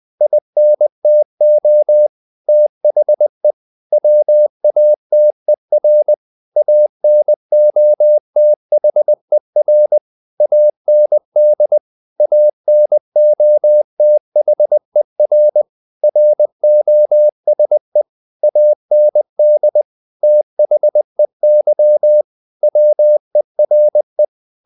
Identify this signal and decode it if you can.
Morse code, heard as a keyed tone, about 20 wpm, 600 Hz. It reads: INTO THE WATER ANOTHER AND ANOTHER ROSE AND THEY WERE
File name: the old man and the sea / the old man and the sea part 1 20wpm